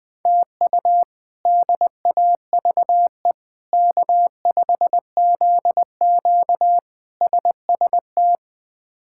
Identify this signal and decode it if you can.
Morse code, heard as a keyed tone, about 20 wpm, 700 Hz. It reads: TU DAVE K5ZQ SST
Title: K1USN SST QSO Practice - 20wpm